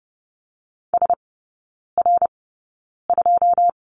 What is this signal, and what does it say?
Morse code, heard as a keyed tone, about 30 wpm, spaced as for 9 wpm, 700 Hz. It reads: SR2
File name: course / 30:10wpm / 087 Lesson 15e - Number 5 - Three Characters 30wpm